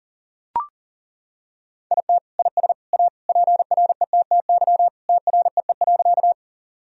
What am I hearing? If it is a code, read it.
Morse code, heard as a keyed tone, about 40 wpm, 700 Hz. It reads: IT IS A PRETTY TREE.
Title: Sentences from Top 1000 Words - Review Part 1 - 40wpm